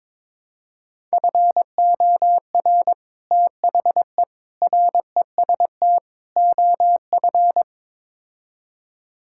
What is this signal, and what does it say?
Morse code, heard as a keyed tone, about 21 wpm, 700 Hz. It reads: FOR THE REST OF
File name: practice / 22wpm / Build A Phrase - 5-Word Phrases - 22wpm